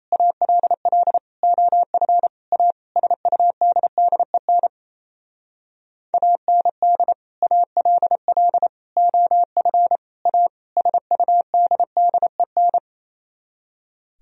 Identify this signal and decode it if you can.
Morse code, heard as a keyed tone, about 34 wpm, 700 Hz. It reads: ALL OF A SUDDEN AND ALL OF A SUDDEN
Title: Speed-Racing - Sets of 5 Words 28wpm